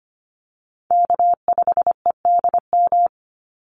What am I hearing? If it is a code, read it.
Morse code, heard as a keyed tone, about 25 wpm, 700 Hz. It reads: K5EDM
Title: Call Signs - 25wpm